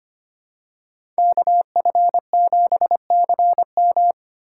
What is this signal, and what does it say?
Morse code, heard as a keyed tone, about 25 wpm, 700 Hz. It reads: KF7CM
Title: USA Call Signs - 5 Characters - 3x Character Spacing - 25wpm